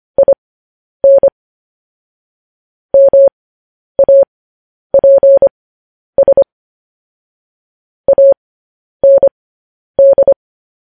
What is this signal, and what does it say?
Morse code, heard as a keyed tone, about 25 wpm, spaced as for 10 wpm, 550 Hz. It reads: IN MAPS AND
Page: news-headlines-in-morse-code-at-10-wpm